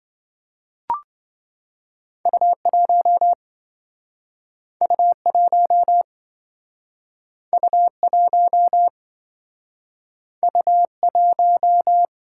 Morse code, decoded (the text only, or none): U1 U1 U1 U1